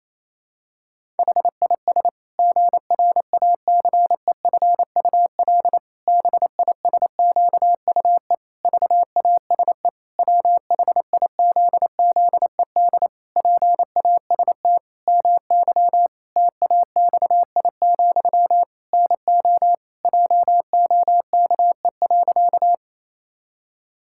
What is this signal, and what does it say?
Morse code, heard as a keyed tone, about 28 wpm, 700 Hz. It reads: HIS GRACEFUL BISQUE VASE WHIZZED PAST MY TAXI, NO JOKE.
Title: Pangram Challenge v2 - 28wpm